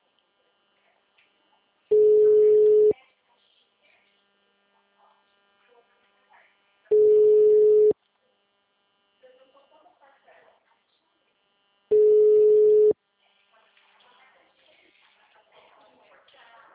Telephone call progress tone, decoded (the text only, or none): ringback tone